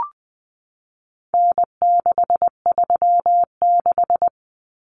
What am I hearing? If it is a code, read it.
Morse code, heard as a keyed tone, about 20 wpm, 700 Hz. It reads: N636
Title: Common Cut Numbers - 4-Digit 20wpm